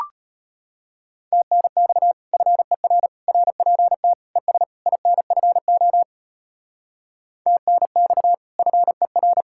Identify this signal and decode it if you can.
Morse code, heard as a keyed tone, about 32 wpm, 700 Hz. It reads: TNX FER RPT ES INFO TNX FER